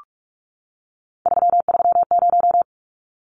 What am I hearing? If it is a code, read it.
Morse code, heard as a keyed tone, about 44 wpm, 700 Hz. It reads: 330
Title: Numbers - 3-Digit 45wpm